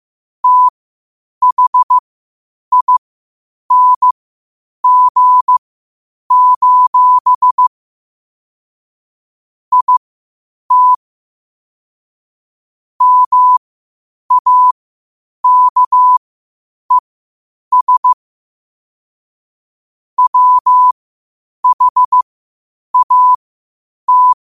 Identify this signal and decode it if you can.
Morse code, heard as a keyed tone, about 15 wpm, spaced as for 8 wpm, 1000 Hz. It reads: THING: IT MAKES WHAT
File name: qotdfiles / 8wpm